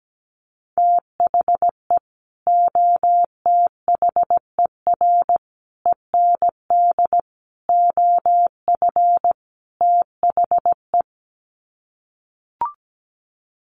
Morse code, heard as a keyed tone, about 17 wpm, 700 Hz. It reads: THE OTHER END OF THE
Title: Sets of 5 Words 17wpm